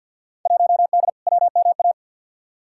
Morse code, heard as a keyed tone, about 51 wpm, 700 Hz. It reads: 1D WMA
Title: Field Day Exchanges - 50wpm